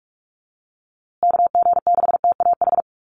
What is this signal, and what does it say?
Morse code, heard as a keyed tone, about 44 wpm, 700 Hz. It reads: KG6TAH
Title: USA Call Signs Repeated 3x 45wpm